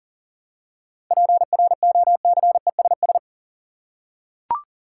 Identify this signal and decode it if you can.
Morse code, heard as a keyed tone, about 40 wpm, 700 Hz. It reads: PROCESS
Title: Seven Letter Words - 40wpm